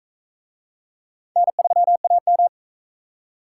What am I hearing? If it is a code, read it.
Morse code, heard as a keyed tone, about 40 wpm, 700 Hz. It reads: N3AM